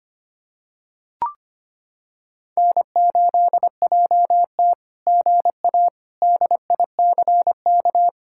Morse code, heard as a keyed tone, about 25 wpm, 700 Hz. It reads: N8JT GA DICK
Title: SOTA QSO Practice 25wpm